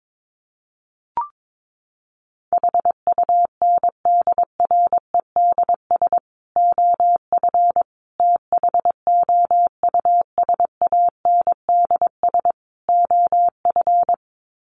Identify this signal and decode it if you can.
Morse code, heard as a keyed tone, about 22 wpm, 700 Hz. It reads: HUNDREDS OF THOUSANDS OF
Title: Sets of 4 Words 22wpm